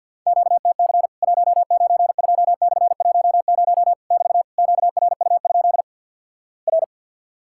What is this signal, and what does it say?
Morse code, heard as a keyed tone, about 50 wpm, 700 Hz. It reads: QTC 192/10 <BT> QRU? R
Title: Worked All Europe DX Contest - QTC Exchange - 50wpm